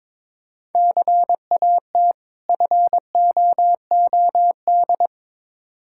Morse code, heard as a keyed tone, about 22 wpm, 700 Hz. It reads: CAT FOOD
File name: practice / 22wpm / Difficult Words with Hints - 22wpm